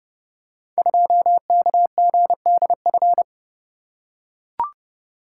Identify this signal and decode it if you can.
Morse code, heard as a keyed tone, about 30 wpm, 690 Hz. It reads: 2KGDF
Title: Code Groups - 5 Letter:Numbers - 30wpm